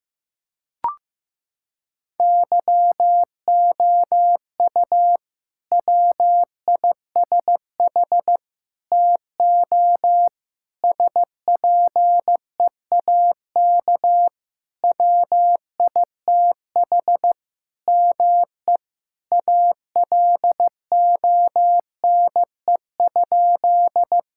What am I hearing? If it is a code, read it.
Morse code, heard as a keyed tone, about 15 wpm, 700 Hz. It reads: YOU WISH TO SPEAK WITH ME ALONE?
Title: Sentences from Top 1600 Words 15wpm